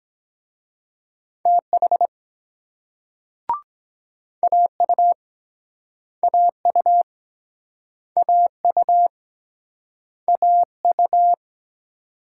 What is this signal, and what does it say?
Morse code, heard as a keyed tone, about 22 wpm, 700 Hz. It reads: TH AU AU AU AU